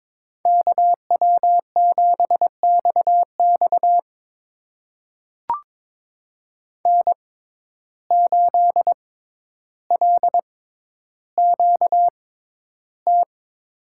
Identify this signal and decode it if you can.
Morse code, heard as a keyed tone, about 22 wpm, spaced as for 11 wpm, 700 Hz. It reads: KW7XX N8LQT
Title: USA Call Signs - 5 Characters - 6x Character Spacing - 22wpm